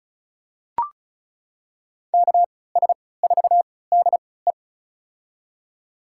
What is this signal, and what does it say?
Morse code, heard as a keyed tone, about 36 wpm, spaced as for 20 wpm, 700 Hz. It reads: KS4DE
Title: USA Call Signs - 5 Characters - 3x Character Spacing - 35wpm